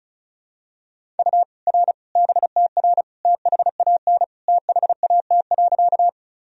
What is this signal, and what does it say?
Morse code, heard as a keyed tone, about 36 wpm, 700 Hz. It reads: U R BTR THAN THAT.
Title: Sentences from Top 400 Words and Abbreviations 35wpm